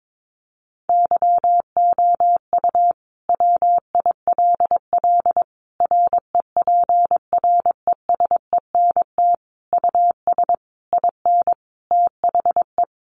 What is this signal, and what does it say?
Morse code, heard as a keyed tone, about 22 wpm, 700 Hz. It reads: YOU WILL REPRESENT US IN THE
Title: Sentences from Top 1200 Words 22wpm